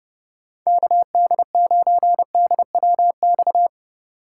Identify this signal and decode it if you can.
Morse code, heard as a keyed tone, about 31 wpm, 700 Hz. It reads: KD9DWX